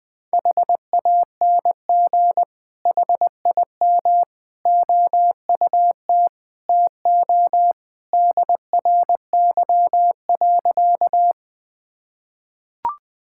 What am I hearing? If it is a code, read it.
Morse code, heard as a keyed tone, about 20 wpm, 700 Hz. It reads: HANG HIM OUT TO DRY.
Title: Sentences from Top 1600 Words 20wpm